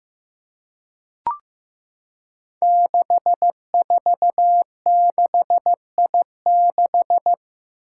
Morse code, heard as a keyed tone, about 15 wpm, 700 Hz. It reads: 646I6
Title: Five Challenging Letter-Number Combinations 15wpm